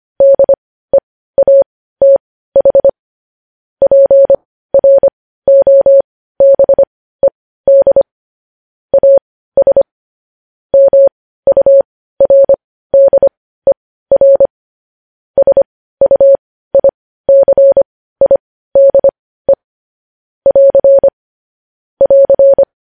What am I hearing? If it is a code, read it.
Morse code, heard as a keyed tone, about 25 wpm, spaced as for 15 wpm, 550 Hz. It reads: DEATH PROBED AS MURDER SUICIDE <AR> <AR>
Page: news headlines in morse code at 15 wpm